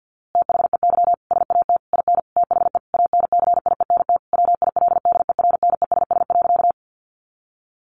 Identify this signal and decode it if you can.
Morse code, heard as a keyed tone, about 49 wpm, 700 Hz. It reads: THEY SAT IN THE ANCIENT WILDERNESS.
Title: A Sound Of Thunder 50wpm